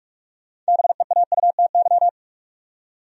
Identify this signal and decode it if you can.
Morse code, heard as a keyed tone, about 57 wpm, 700 Hz. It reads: BEAUTY